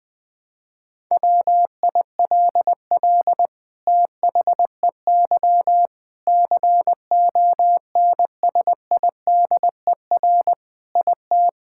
Morse code, heard as a keyed tone, about 21 wpm, 700 Hz. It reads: WILL THEY CONSIDER IT
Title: Sentences from Top 1000 Words - Review Part 5 - 20wpm